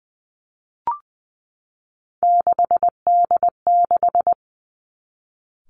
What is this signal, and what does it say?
Morse code, heard as a keyed tone, about 20 wpm, 700 Hz. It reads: 6D6
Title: Three Challenging Letter-Number Combinations 20wpm